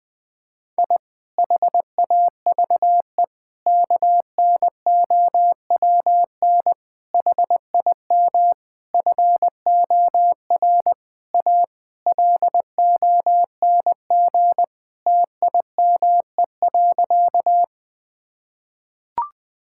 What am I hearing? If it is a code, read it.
Morse code, heard as a keyed tone, about 20 wpm, 700 Hz. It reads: I HAVE KNOWN HIM FOR A LONG TIME.
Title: Sentences with 4-Word Phrases - 20wpm